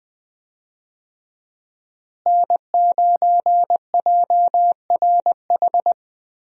Morse code, heard as a keyed tone, about 20 wpm, 700 Hz. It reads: N9JRH
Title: USA Call Signs - 5 Characters - Repeated 2x - 20wpm